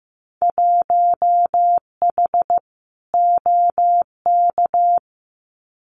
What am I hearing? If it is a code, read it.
Morse code, heard as a keyed tone, about 15 wpm, 700 Hz. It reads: 1H OK
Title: Winter Field Day Exchanges 15wpm